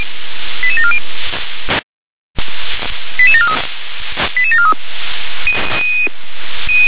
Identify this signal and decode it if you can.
Five-tone selective call (ZVEI), heard as quick runs of five tones, tones of about 80 ms. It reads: 88740 88743 88743